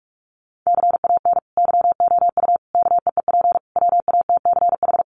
Morse code, heard as a keyed tone, about 44 wpm, 700 Hz. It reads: CAN YOU KEEP WATCH
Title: Sentences from Top 1100 Words 45wpm